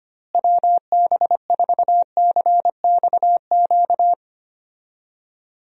Morse code, heard as a keyed tone, about 25 wpm, 700 Hz. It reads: WB4CXQ
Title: USA Call Signs - 6 Characters - Repeated 2x - 25wpm